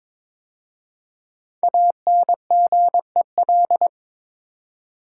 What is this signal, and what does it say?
Morse code, heard as a keyed tone, about 22 wpm, 700 Hz. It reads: ANGEL
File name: 5 Letter Words - First 3 Letters Given - 22wpm